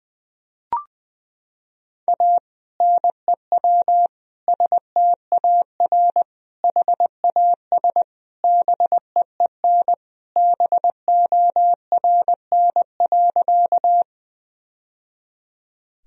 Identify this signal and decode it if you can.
Morse code, heard as a keyed tone, about 20 wpm, 700 Hz. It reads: A NEW STAR HAS BEEN BORN.
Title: Sentences from Top 1000 Words - Review Part 2 - 20wpm